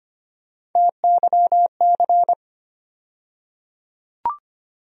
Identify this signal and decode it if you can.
Morse code, heard as a keyed tone, about 25 wpm, 700 Hz. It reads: TYC
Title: USA Call Sign Postfixes - 3 Letters 25wpm